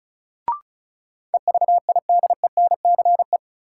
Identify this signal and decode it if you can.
Morse code, heard as a keyed tone, about 36 wpm, 700 Hz. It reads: EVIDENCE
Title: Sentences from Top 1300 Words - New Word Drill Alt - 35wpm